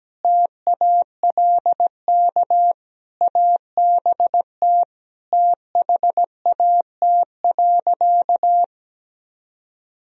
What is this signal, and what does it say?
Morse code, heard as a keyed tone, about 17 wpm, 700 Hz. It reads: TALK ABT THAT.
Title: Sentences from Top 400 Words and Abbreviations 17wpm